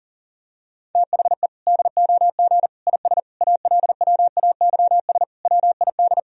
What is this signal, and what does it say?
Morse code, heard as a keyed tone, about 40 wpm, 700 Hz. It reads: THE DOG IS ALWAYS WID
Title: Sentences from Top 400 Words and Abbreviations 40wpm